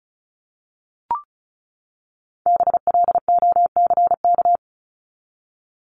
Morse code, heard as a keyed tone, about 34 wpm, 700 Hz. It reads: BLOCK
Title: Top 1501-1750 Words 35wpm